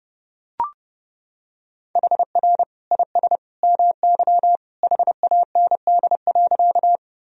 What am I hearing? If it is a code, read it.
Morse code, heard as a keyed tone, about 32 wpm, 700 Hz. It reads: HR IS MY HAND.
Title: Sentences from Top 200 Words and Abbreviations 30wpm